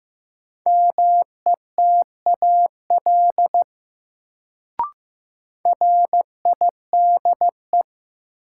Morse code, heard as a keyed tone, about 15 wpm, 700 Hz. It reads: METAL RIDE
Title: Sentences from Top 1000 Words - Review Part 6 - New Word Drill - 15wpm